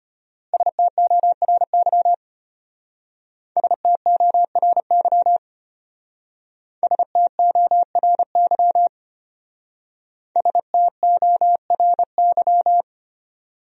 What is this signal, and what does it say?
Morse code, heard as a keyed tone, about 32 wpm, 700 Hz. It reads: STORY STORY STORY STORY